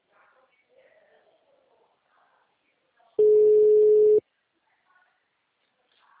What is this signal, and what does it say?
Ringback tone (Europe, 425 Hz).